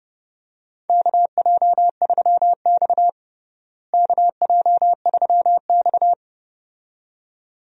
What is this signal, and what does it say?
Morse code, heard as a keyed tone, about 30 wpm, 700 Hz. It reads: KJ3X KJ3X
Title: USA Call Signs - 4 Characters - Repeated 2x - 30wpm